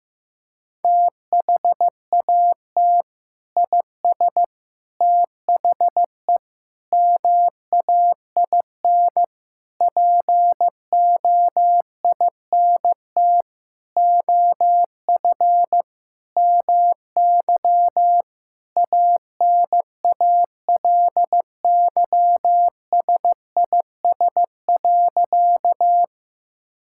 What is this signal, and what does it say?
Morse code, heard as a keyed tone, about 15 wpm, 700 Hz. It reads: THAT IS THE MAIN POINT OF MY ANALYSIS.
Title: Sentences from Top 1000 Words - Review Part 5 - 15wpm